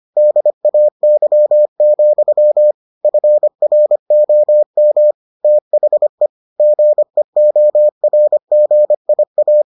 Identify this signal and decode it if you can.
Morse code, heard as a keyed tone, about 25 wpm, 600 Hz. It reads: DAY, FROM THE GEORGIA